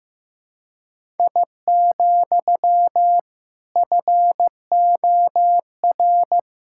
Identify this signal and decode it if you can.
Morse code, heard as a keyed tone, about 15 wpm, 700 Hz. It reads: I, FOR
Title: Sentences from Top 1000 Words - Review Part 5 - 15wpm